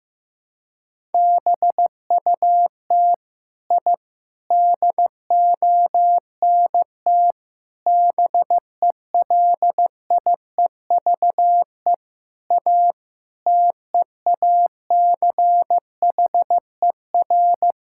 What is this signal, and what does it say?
Morse code, heard as a keyed tone, about 15 wpm, 700 Hz. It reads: BUT I DONT BELIEVE A TEACHER
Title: Sentences from Top 1000 Words - Review Part 6 - 15wpm